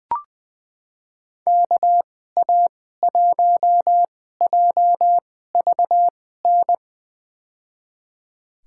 Morse code, heard as a keyed tone, about 20 wpm, spaced as for 14 wpm, 700 Hz. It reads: KA1JVN
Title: USA Call Signs - 6 Characters - 2x Character Spacing - 20wpm